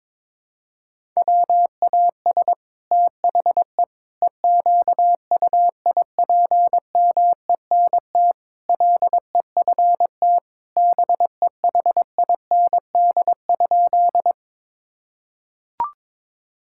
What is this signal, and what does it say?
Morse code, heard as a keyed tone, about 22 wpm, 700 Hz. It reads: WAS THE EQUIPMENT LEFT BEHIND?
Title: Sentences from Top 1000 Words - Review Part 5 - 22wpm